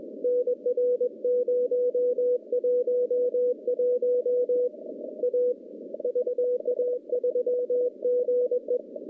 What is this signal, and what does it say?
Morse code, heard as a keyed tone, about 21 wpm, 490 Hz. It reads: NR 011 A VA3GE
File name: contests / CW-SS-k5tr- -0400